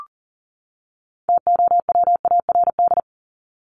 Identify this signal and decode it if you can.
Morse code, heard as a keyed tone, about 40 wpm, 700 Hz. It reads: TOWARD
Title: ES:S Suffix - Top 100 Words with Base Words 40wpm